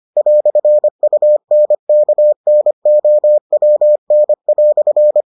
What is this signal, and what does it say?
Morse code, heard as a keyed tone, about 25 wpm, 600 Hz. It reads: "UNKNOWN"